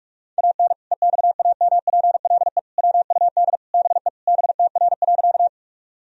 Morse code, heard as a keyed tone, about 46 wpm, 700 Hz. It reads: AN EXAMPLE WUD BE BTR.
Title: Sentences from Top 400 Words and Abbreviations 45wpm